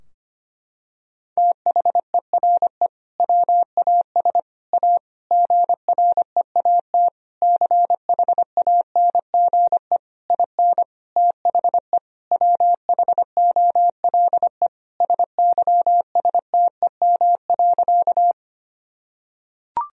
Morse code, heard as a keyed tone, about 25 wpm, 700 Hz. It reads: THERE WAS A GREAT CHANGE IN THE WHOLE SYSTEM.
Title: Sentences from Top 1000 Words - Review Part 3 - 25wpm